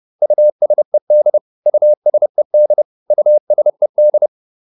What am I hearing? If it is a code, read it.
Morse code, heard as a keyed tone, about 30 wpm, 600 Hz. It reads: USED USED USED
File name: ED4 30